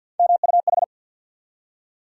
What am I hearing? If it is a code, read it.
Morse code, heard as a keyed tone, about 52 wpm, 700 Hz. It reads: MUH
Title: USA Call Sign Postfixes - 3 Letters 50wpm